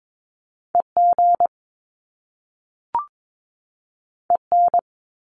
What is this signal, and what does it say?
Morse code, heard as a keyed tone, about 22 wpm, 700 Hz. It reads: EG EN
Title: Top Two Letter Combinations in Words 22wpm